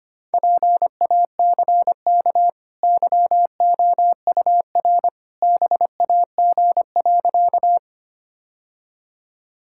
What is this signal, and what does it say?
Morse code, heard as a keyed tone, about 25 wpm, 690 Hz. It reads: PACK YOUR BAG.
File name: Easy Sentences with Hints - 25wpm